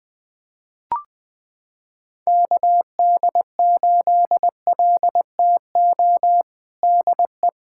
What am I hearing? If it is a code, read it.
Morse code, heard as a keyed tone, about 21 wpm, 700 Hz. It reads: KD8LTO DE
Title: USA Callsign DE Callsign - 6 Characters - 20wpm